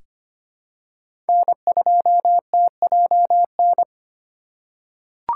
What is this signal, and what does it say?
Morse code, heard as a keyed tone, about 25 wpm, 700 Hz. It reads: N2TJN
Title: USA Call Signs - 3x Character Spacing - 25wpm